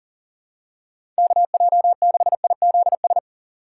Morse code, heard as a keyed tone, about 40 wpm, 700 Hz. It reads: KJ6IZS
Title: USA Call Signs - 4x Character Spacing - 40wpm